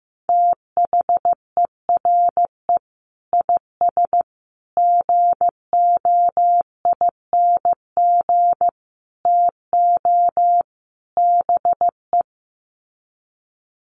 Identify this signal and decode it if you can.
Morse code, heard as a keyed tone, about 15 wpm, 700 Hz. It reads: THERE IS GOING TO BE
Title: Sets of 5 Words Encore 15wpm